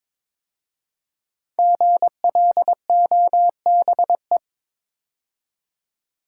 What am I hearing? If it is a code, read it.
Morse code, heard as a keyed tone, about 22 wpm, 700 Hz. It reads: GLOBE
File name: 5 Letter Words - First 3 Letters Given - 22wpm